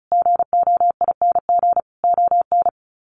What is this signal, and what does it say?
Morse code, heard as a keyed tone, about 34 wpm, 700 Hz. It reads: GOING ON
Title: Sets of 5 Words Encore2 35wpm